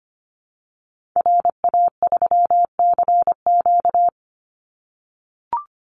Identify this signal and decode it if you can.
Morse code, heard as a keyed tone, about 25 wpm, 700 Hz. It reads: RA3CQ
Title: Call Signs - 25wpm